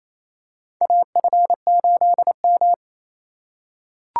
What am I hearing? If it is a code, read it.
Morse code, heard as a keyed tone, about 28 wpm, 700 Hz. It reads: AF8M
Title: USA Call Signs - 4 Characters - 2x Character Spacing - 28wpm